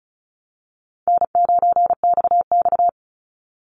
Morse code, heard as a keyed tone, about 34 wpm, 700 Hz. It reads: N9XX
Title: Easy Call Signs Repeated 3x - 35wpm